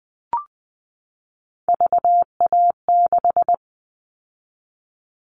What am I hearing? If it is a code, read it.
Morse code, heard as a keyed tone, about 20 wpm, 700 Hz. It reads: VA6
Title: Callsign Prefixes for Top 100 DXCC - 20wpm